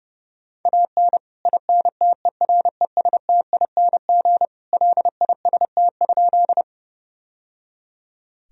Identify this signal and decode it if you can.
Morse code, heard as a keyed tone, about 30 wpm, 700 Hz. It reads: AN INTERESTING LIST?
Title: Sentences from Top 1000 Words - Review Part 2 - 30wpm